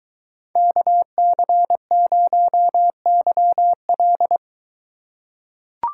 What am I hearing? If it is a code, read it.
Morse code, heard as a keyed tone, about 23 wpm, 700 Hz. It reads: KC0YL